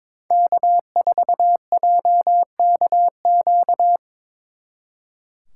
Morse code, heard as a keyed tone, about 25 wpm, 700 Hz. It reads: K4JKQ
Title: Speed Racing - USA 1x3 Call Signs 22wpm